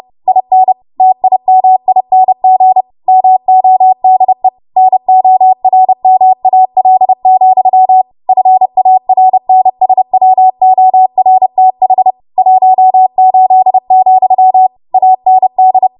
Morse code, heard as a keyed tone, about 30 wpm, 750 Hz. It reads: IN TIMING MODE NORMAL, FARNSWORTH 18, AND
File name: Morse / 30WPM